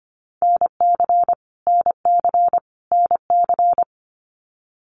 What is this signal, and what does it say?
Morse code, heard as a keyed tone, about 24 wpm, 700 Hz. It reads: NC NC NC
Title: US States Repeated 3x 25wpm